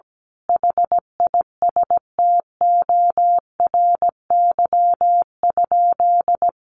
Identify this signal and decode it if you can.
Morse code, heard as a keyed tone, about 17 wpm, 700 Hz. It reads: HISTORY?
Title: Sentences from Top 600 Words 17wpm